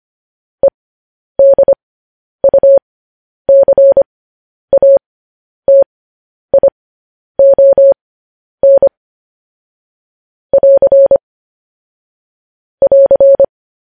Morse code, heard as a keyed tone, about 25 wpm, spaced as for 10 wpm, 550 Hz. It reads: EDUCATION <AR> <AR>
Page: News Headlines in Morse Code at 10 WPM